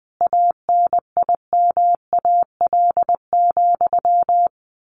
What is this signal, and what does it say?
Morse code, heard as a keyed tone, about 20 wpm, 700 Hz. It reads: ANIMAL,
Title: A Sound Of Thunder 20wpm